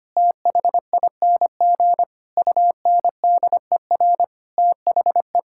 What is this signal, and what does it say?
Morse code, heard as a keyed tone, about 25 wpm, 700 Hz. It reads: THING UNDER THE
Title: Sentences from Top 1000 Words - Review Part 5 - 25wpm